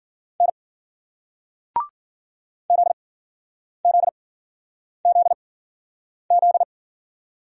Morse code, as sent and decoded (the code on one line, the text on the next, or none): -. / --.. --.. --.. --..
N ZZZZ